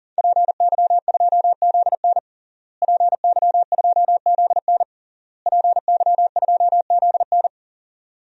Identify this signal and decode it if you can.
Morse code, heard as a keyed tone, about 40 wpm, 700 Hz. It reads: PY2ZN PY2ZN PY2ZN
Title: DXCC-100 Call Signs - 5 Characters - Repeated 3x - 40wpm